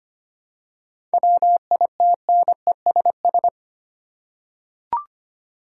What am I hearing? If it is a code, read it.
Morse code, heard as a keyed tone, about 25 wpm, 700 Hz. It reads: WITNESS
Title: Seven Letter Words - 25wpm